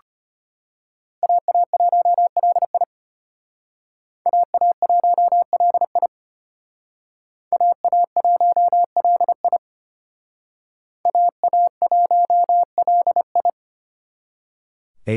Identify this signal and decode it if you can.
Morse code, heard as a keyed tone, about 32 wpm, 700 Hz. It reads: AA1LI AA1LI AA1LI AA1LI